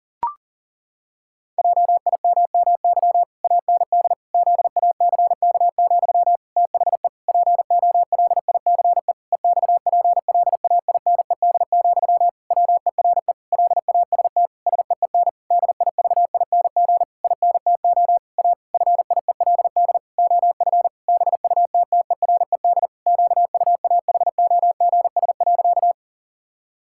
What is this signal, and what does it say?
Morse code, heard as a keyed tone, about 40 wpm, 700 Hz. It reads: JIMMY AND ZACK, THE POLICE EXPLAINED, WERE LAST SEEN DIVING INTO A FIELD OF BUTTERED QUAHOGS.
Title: Pangram Challenge v2 - 40wpm